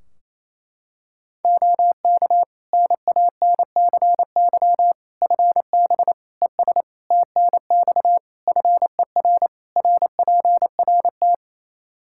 Morse code, heard as a keyed tone, about 28 wpm, 700 Hz. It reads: OK NANCY FB ES TNX FER RPRT
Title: Practice QSO - Exchange 2 - Info - 28wpm